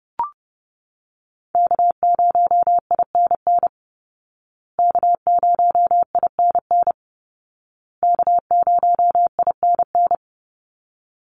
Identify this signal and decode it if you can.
Morse code, heard as a keyed tone, about 30 wpm, 700 Hz. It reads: K0INN K0INN K0INN
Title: Call Signs Repeated 2x - 30wpm